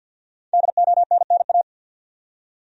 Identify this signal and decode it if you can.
Morse code, heard as a keyed tone, about 46 wpm, 700 Hz. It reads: DONNA
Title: USA Common Names - Max 5 Letters - 50wpm